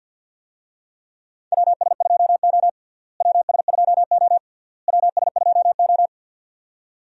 Morse code, heard as a keyed tone, about 53 wpm, 700 Hz. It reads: WS2O WS2O WS2O
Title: USA Call Signs - 4 Characters - Repeated 3x - 50wpm